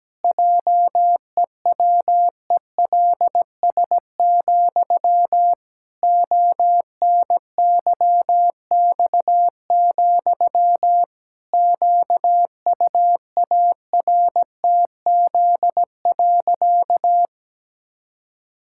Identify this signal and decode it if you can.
Morse code, heard as a keyed tone, about 17 wpm, 700 Hz. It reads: JEWELS, ONYX, QUARTZ.
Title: Pangram Challenge v2 - 17wpm